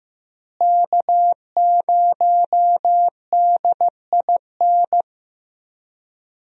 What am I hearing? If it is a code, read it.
Morse code, heard as a keyed tone, about 15 wpm, 700 Hz. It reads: K0DIN
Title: USA Call Signs - 5 Characters - Repeated 3x - 15wpm